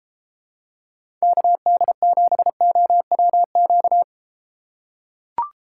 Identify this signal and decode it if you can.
Morse code, heard as a keyed tone, about 33 wpm, 700 Hz. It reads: KD7OWQ